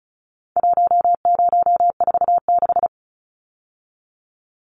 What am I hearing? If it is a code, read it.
Morse code, heard as a keyed tone, about 34 wpm, 700 Hz. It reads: 1046
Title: Numbers - 4-Digit 35wpm